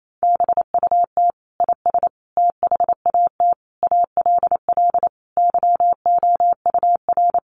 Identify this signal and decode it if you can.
Morse code, heard as a keyed tone, about 28 wpm, 700 Hz. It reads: BUT IS THAT ALL YOUR
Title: Sentences from Top 1300 Words 28wpm